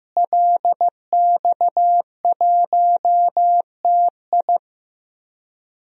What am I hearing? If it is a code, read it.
Morse code, heard as a keyed tone, about 15 wpm, 700 Hz. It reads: LX1TI
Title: DXCC-100 Call Signs - 5 Characters - Repeated 3x - 15wpm